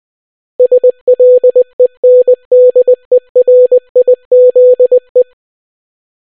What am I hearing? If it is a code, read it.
Morse code, heard as a keyed tone, about 20 wpm, 500 Hz. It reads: SLENDERIZE